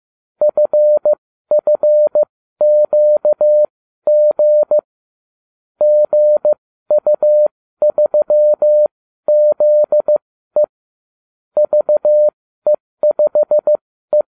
Morse code, heard as a keyed tone, about 15 wpm, spaced as for 12 wpm, 600 Hz. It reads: FFQG GU3ZE VE5E